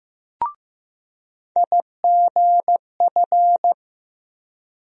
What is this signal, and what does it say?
Morse code, heard as a keyed tone, about 15 wpm, 700 Hz. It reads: IGF
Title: USA Call Sign Postfixes - 3 Letters 15wpm